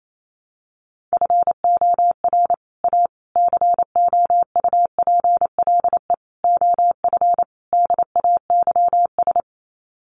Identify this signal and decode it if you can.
Morse code, heard as a keyed tone, about 28 wpm, 700 Hz. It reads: FOR A COUPLE OF DAYS
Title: Sets of 5 Words Encore2 28wpm